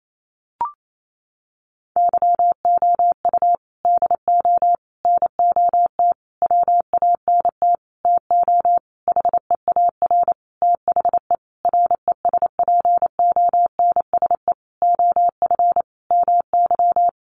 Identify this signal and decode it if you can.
Morse code, heard as a keyed tone, about 28 wpm, 700 Hz. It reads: YOU DO NOT WANT TO HEAR THE RESPONSE OF MY
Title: Sentences from Top 1400 Words 28wpm